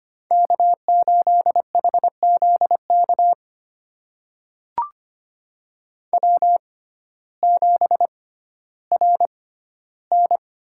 Morse code, heard as a keyed tone, about 25 wpm, spaced as for 14 wpm, 700 Hz. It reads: K8HZK W7RN
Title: USA Call Signs - 5 Characters - 6x Character Spacing - 25wpm